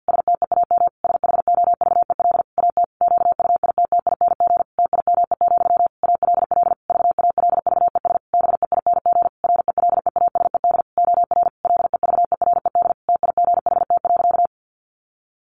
Morse code, heard as a keyed tone, about 50 wpm, 700 Hz. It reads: STEAM SHOVEL AT QUITTING TIME, ALL VALVES BEING RELEASED OR LEVERED TIGHT.
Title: A Sound Of Thunder 50wpm